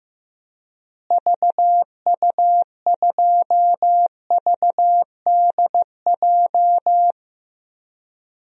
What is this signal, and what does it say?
Morse code, heard as a keyed tone, about 15 wpm, 700 Hz. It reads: VU2VDJ
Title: DXCC-100 Call Signs - 6 Characters - Repeated 2x - 15wpm